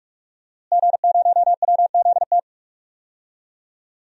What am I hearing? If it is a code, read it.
Morse code, heard as a keyed tone, about 43 wpm, 700 Hz. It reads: G0WZT
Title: DXCC-100 Call Signs - 5 Characters - Repeated 2x - 45wpm